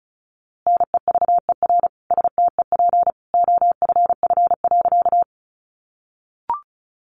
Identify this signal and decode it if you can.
Morse code, heard as a keyed tone, about 34 wpm, 700 Hz. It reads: NEVER STEP OFF.
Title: A Sound Of Thunder 35wpm